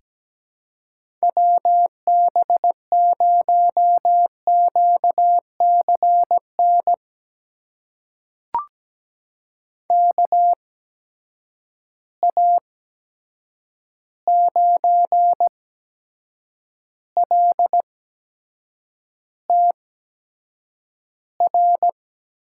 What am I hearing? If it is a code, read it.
Morse code, heard as a keyed tone, about 17 wpm, spaced as for 7 wpm, 700 Hz. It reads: WB0QCN KA9LTR
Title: USA Call Signs - 6 Characters - 8x Character Spacing - 17wpm